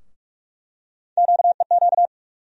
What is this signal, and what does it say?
Morse code, heard as a keyed tone, about 43 wpm, 700 Hz. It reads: QEQ
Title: USA Call Sign Postfixes - 3 Letters 45wpm